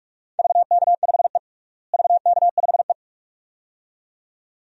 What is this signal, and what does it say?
Morse code, heard as a keyed tone, about 42 wpm, 700 Hz. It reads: VK5E VK5E
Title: DXCC-100 Call Signs - 4 Characters - Repeated 2x - 45wpm